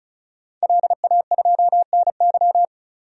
Morse code, heard as a keyed tone, about 36 wpm, 700 Hz. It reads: LA2NY
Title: DXCC-100 Call Signs - 5 Characters - 35wpm